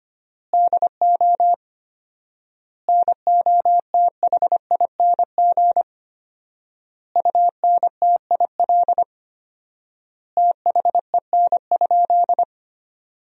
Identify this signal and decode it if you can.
Morse code, heard as a keyed tone, about 25 wpm, 700 Hz. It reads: DO NOTHING UNTIL THEN?
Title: 4x Word-Spacing - Sentences from Top 500 Words - 25wpm